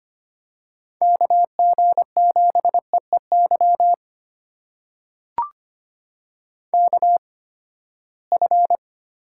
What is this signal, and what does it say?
Morse code, heard as a keyed tone, about 25 wpm, spaced as for 17 wpm, 700 Hz. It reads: KG7EEY KF
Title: USA Call Signs - 8x Character Spacing - 25wpm